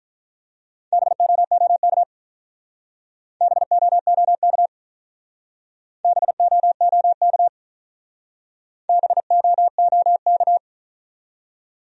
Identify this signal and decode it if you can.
Morse code, heard as a keyed tone, about 44 wpm, 700 Hz. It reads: BOOK BOOK BOOK BOOK
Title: Speed-Racing Top 300 Words 35wpm